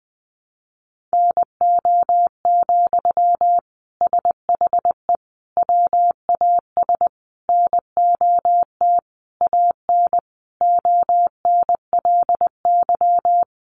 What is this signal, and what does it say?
Morse code, heard as a keyed tone, about 20 wpm, 700 Hz. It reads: NO, SHE WAS NOT AN ONLY
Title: Sentences from Top 800 Words 20wpm